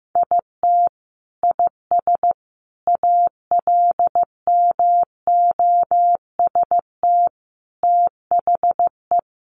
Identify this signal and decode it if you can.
Morse code, heard as a keyed tone, about 15 wpm, 700 Hz. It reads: IT IS ALMOST THE
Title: Sentences from Top 1400 Words 15wpm